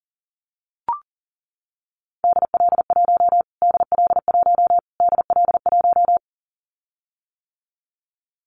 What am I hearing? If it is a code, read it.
Morse code, heard as a keyed tone, about 40 wpm, 700 Hz. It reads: DL1 DL1 DL1
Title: Callsign Prefixes for Top 25 DXCC - Repeated 3x - 40wpm